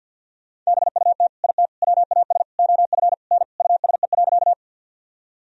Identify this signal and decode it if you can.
Morse code, heard as a keyed tone, about 51 wpm, 700 Hz. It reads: BUT IT WAS OF N USE.
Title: Sentences from Top 100 Words and Abbreviations 50wpm